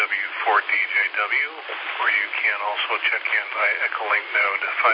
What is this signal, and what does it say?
Emergency Alert System (SAME) tones, heard as harsh buzzing bursts, with speech. It is heard from 0.7 to 3.9 s.